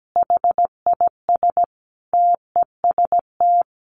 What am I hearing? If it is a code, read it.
Morse code, heard as a keyed tone, about 17 wpm, 700 Hz. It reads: HIS TEST
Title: Additional Easy Phrases 17wpm